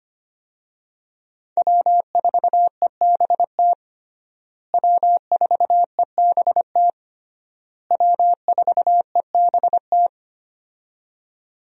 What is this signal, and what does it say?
Morse code, heard as a keyed tone, about 25 wpm, 700 Hz. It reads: W4EBT W4EBT W4EBT
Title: USA Call Signs - 5 Characters - Repeated 3x - 25wpm